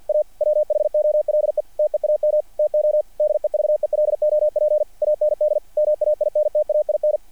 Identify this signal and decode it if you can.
Morse code, heard as a keyed tone, about 48 wpm, 660 Hz. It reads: A WHOLE TEAM TO DEVELOP AND MAINTAIN
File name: archived / CRSnet